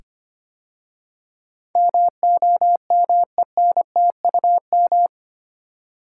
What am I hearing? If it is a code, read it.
Morse code, heard as a keyed tone, about 25 wpm, 700 Hz. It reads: MOMENTUM
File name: 8 Letter Words - First 3 Letters Given - 25wpm